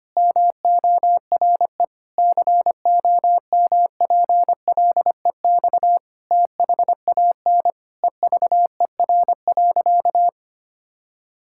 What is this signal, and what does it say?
Morse code, heard as a keyed tone, about 25 wpm, 700 Hz. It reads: MORE COMPLEX THAN EVER.
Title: Sentences from Top 1000 Words - Review Part 6 - 25wpm